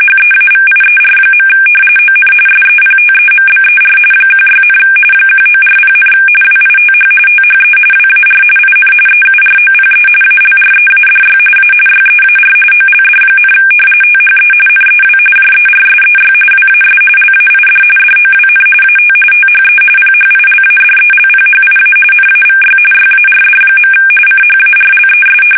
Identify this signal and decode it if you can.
RTTY, about 45.45 baud. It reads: HEAVIER. ON SEPTEMBER 16, 1940, WITH THE WAR IN EUROPE ALREADY A YEAR OLD, THE NEW JERSEY WAS LAID DOWN AT THE PHILADELPHIA NAVY YARD. CHARLES
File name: fsk850 / battleshipnj